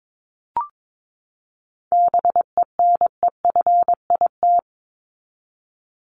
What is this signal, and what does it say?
Morse code, heard as a keyed tone, about 22 wpm, 700 Hz. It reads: BENEFIT
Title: Top 751-1000 Words 22wpm